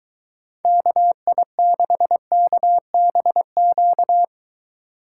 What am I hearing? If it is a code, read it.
Morse code, heard as a keyed tone, about 23 wpm, 700 Hz. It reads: KI6KBQ